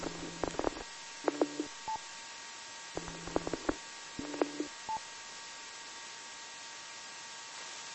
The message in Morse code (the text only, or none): none